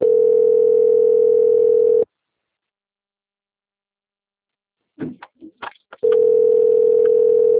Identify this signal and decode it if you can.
Ringback tone (North America), 440 + 480 Hz.